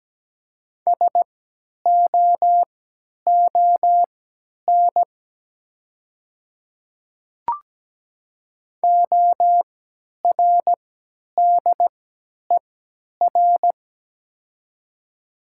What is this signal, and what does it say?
Morse code, heard as a keyed tone, about 17 wpm, spaced as for 10 wpm, 700 Hz. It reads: SOON ORDER
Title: 3x Character Spacing - Top 500 Words - 17wpm